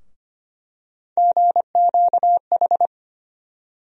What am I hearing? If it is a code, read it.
Morse code, heard as a keyed tone, about 25 wpm, 700 Hz. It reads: GQH
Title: USA Call Sign Postfixes - 3 Letters 25wpm